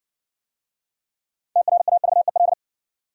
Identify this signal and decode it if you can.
Morse code, heard as a keyed tone, about 60 wpm, 700 Hz. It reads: TRAVEL